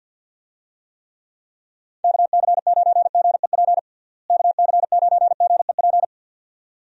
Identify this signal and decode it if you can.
Morse code, heard as a keyed tone, about 51 wpm, 700 Hz. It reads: KC9GEP KC9GEP
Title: USA Call Signs - 6 Characters - Repeated 2x - 50wpm